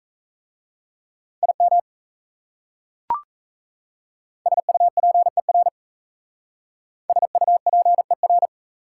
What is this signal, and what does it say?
Morse code, heard as a keyed tone, about 35 wpm, 700 Hz. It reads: IM SUPER SUPER
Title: Speed-Racing Prefixes and Suffixes 28wpm